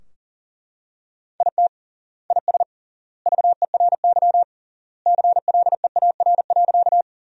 Morse code, heard as a keyed tone, about 40 wpm, 700 Hz. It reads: IT IS VERY CLEAR.
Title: 3x Word-Spacing - Sentences from Top 500 Words - 40wpm